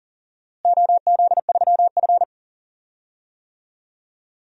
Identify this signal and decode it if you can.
Morse code, heard as a keyed tone, about 39 wpm, 700 Hz. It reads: OZ3F
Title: DXCC-100 Call Signs - 4 Characters - Repeated 3x - 40wpm